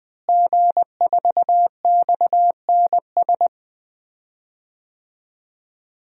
Morse code, heard as a keyed tone, about 20 wpm, 700 Hz. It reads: G4XNS
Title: DXCC-100 Call Signs - 5 Characters - Repeated 2x - 20wpm